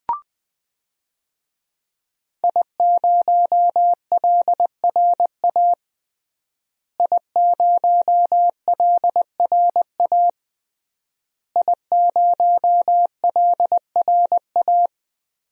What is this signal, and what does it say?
Morse code, heard as a keyed tone, about 20 wpm, 700 Hz. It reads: I0LRA I0LRA I0LRA
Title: DXCC-100 Call Signs - 5 Characters - Repeated 3x - 20wpm